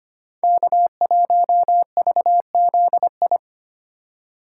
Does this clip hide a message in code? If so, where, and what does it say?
Yes: Morse code from 0.4 to 3.4 s, heard as a keyed tone, about 31 wpm, 700 Hz. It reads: K1VZI